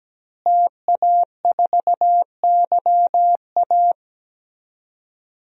Morse code, heard as a keyed tone, about 17 wpm, 700 Hz. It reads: TA4YA
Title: DXCC-100 Call Signs - 5 Characters - Repeated 3x - 17wpm